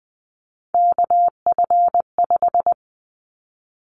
Morse code, heard as a keyed tone, about 20 wpm, 700 Hz. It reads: KF5
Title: Callsign Prefixes for Top 25 DXCC - Repeated 3x - 20wpm